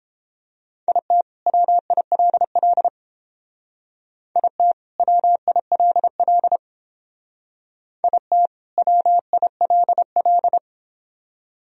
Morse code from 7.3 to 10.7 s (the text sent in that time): IT WILL